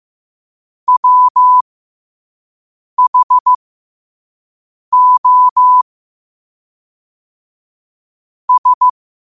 Morse code, heard as a keyed tone, about 15 wpm, spaced as for 5 wpm, 1000 Hz. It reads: WHO S
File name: qotdfiles / 5wpm